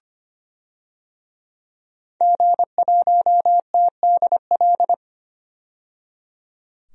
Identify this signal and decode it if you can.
Morse code, heard as a keyed tone, about 25 wpm, 700 Hz. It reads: G1TDL
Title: DXCC-100 Call Signs - 5 Characters - 25wpm